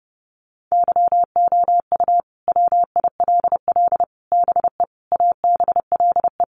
Morse code, heard as a keyed tone, about 28 wpm, 700 Hz. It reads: YOU WILL BE ABLE
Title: Sets of 5 Words Encore2 30wpm